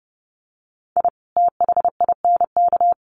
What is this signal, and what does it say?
Morse code, heard as a keyed tone, about 30 wpm, 700 Hz. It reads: I THINK
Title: Sets of 5 Words 30wpm